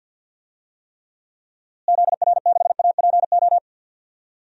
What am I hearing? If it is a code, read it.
Morse code, heard as a keyed tone, about 51 wpm, 700 Hz. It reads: ZR6APO
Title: DXCC-100 Call Signs - 6 Characters - Repeated 3x - 50wpm